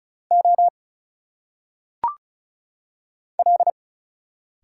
Morse code, heard as a keyed tone, about 33 wpm, 700 Hz. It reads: O L
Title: Single Letters - Phonetic Pronunciation 35wpm